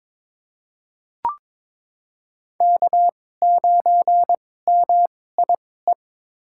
Morse code, heard as a keyed tone, about 21 wpm, spaced as for 16 wpm, 700 Hz. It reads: K9MIE
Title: USA Call Signs - 2x Character Spacing - 22wpm